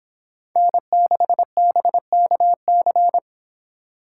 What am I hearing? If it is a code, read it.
Morse code, heard as a keyed tone, about 26 wpm, 700 Hz. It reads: N6BKC